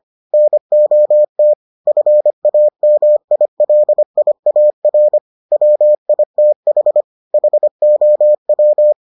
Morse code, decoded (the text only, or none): NOT FAMILIAR WITH HOW